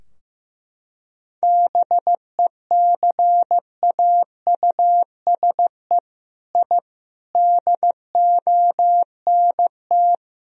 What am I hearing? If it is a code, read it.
Morse code, heard as a keyed tone, about 15 wpm, 700 Hz. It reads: BECAUSE I DONT
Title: Sentences from Top 1000 Words - Review Part 3 - 15wpm